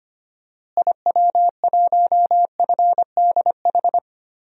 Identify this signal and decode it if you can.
Morse code, heard as a keyed tone, about 25 wpm, 690 Hz. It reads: IW1FDH
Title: DXCC-100 Call Signs - 6 Characters - Repeated 3x - 25wpm